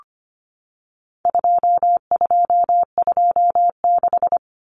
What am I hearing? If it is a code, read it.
Morse code, heard as a keyed tone, about 25 wpm, 700 Hz. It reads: 2226
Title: Common Cut Numbers - 4-Digit 25wpm